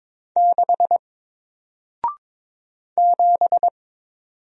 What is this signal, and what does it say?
Morse code, heard as a keyed tone, about 22 wpm, 700 Hz. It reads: 6 7
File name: Single-Letter-Number-22wpm